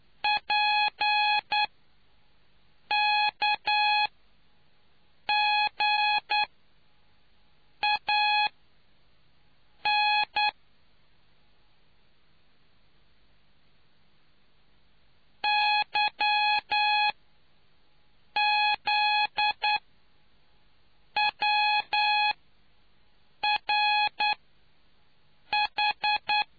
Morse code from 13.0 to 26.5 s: YZWRH